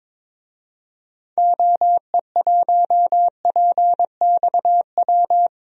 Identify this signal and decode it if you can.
Morse code, heard as a keyed tone, about 22 wpm, 700 Hz. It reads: OE1PXW
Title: DXCC-100 Call Signs - 6 Characters - Repeated 2x - 22wpm